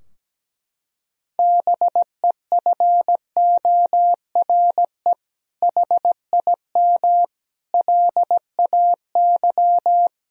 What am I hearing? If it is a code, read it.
Morse code, heard as a keyed tone, about 17 wpm, 700 Hz. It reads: BEFORE HIM LAY
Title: Sentences from Top 1000 Words - Review Part 2 - 17wpm